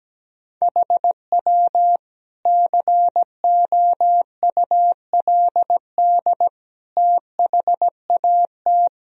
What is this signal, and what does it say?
Morse code, heard as a keyed tone, about 17 wpm, 700 Hz. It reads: HW COULD THAT
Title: Sentences from Top 400 Words and Abbreviations 17wpm